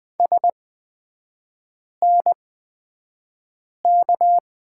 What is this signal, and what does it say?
Morse code, heard as a keyed tone, about 20 wpm, spaced as for 5 wpm, 700 Hz. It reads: SNK
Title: 243 Lesson 39e - Letter z - Three Characters 20wpm